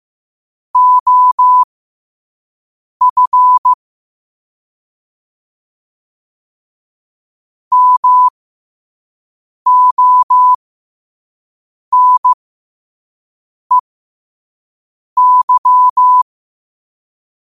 Morse code, heard as a keyed tone, about 15 wpm, spaced as for 5 wpm, 1000 Hz. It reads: OF MONEY
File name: qotdfiles / 5wpm